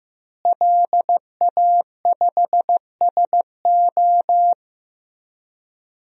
Morse code, heard as a keyed tone, about 15 wpm, 700 Hz. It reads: LA5SO
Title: DXCC-100 Call Signs - 5 Characters - Repeated 2x - 15wpm